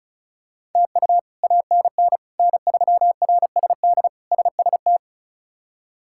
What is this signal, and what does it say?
Morse code, heard as a keyed tone, about 36 wpm, 700 Hz. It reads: TU ANN N3RSD SST
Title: K1USN SST QSO Practice - 35wpm